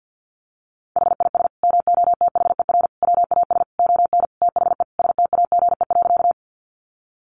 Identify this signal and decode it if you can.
Morse code, heard as a keyed tone, about 50 wpm, 700 Hz. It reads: HIS MOTHER WAS ON THE STAGE.
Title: Sentences from Top 1200 Words 50wpm